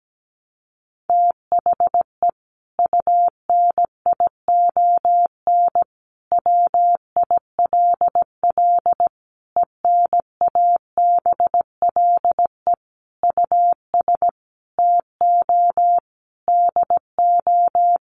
Morse code, heard as a keyed tone, about 17 wpm, 700 Hz. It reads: THE UNION WILL ENABLE US TO DO
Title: Sentences from Top 1500 Words 17wpm